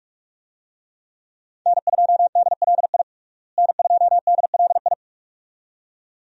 Morse code, heard as a keyed tone, about 41 wpm, 700 Hz. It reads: N2DLI N2DLI
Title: USA Call Signs - 5 Characters - Repeated 2x - 45wpm